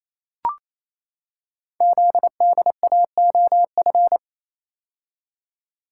Morse code, heard as a keyed tone, about 29 wpm, 700 Hz. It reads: ZDAOF
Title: Code Groups - 5 Letter:Numbers - 28wpm